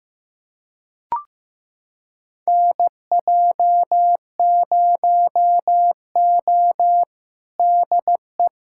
Morse code, heard as a keyed tone, about 15 wpm, 700 Hz. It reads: NJ0O DE
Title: USA Callsign DE Callsign - 4 Characters - 15wpm